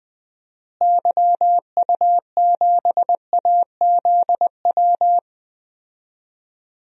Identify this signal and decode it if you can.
Morse code, heard as a keyed tone, about 21 wpm, 700 Hz. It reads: YU7AZW
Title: DXCC-100 Call Signs - 6 Characters - Repeated 3x - 20wpm